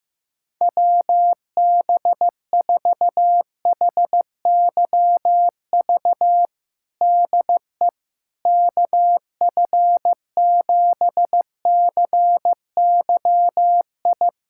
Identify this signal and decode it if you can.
Morse code, heard as a keyed tone, about 15 wpm, 700 Hz. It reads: WB4HYV DE KF7CYI
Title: USA Callsign DE Callsign - 6 Characters - 15wpm